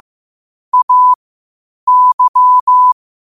Morse code, heard as a keyed tone, about 15 wpm, spaced as for 8 wpm, 1000 Hz. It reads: AY
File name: qotdfiles / 8wpm